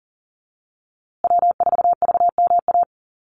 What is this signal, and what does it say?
Morse code, heard as a keyed tone, about 38 wpm, 700 Hz. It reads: W4VMA
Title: USA Call Signs Repeated 2x 40wpm